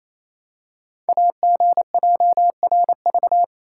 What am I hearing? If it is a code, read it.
Morse code, heard as a keyed tone, about 27 wpm, 700 Hz. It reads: AGJRV
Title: Code Groups - 5 Letter:Numbers - 28wpm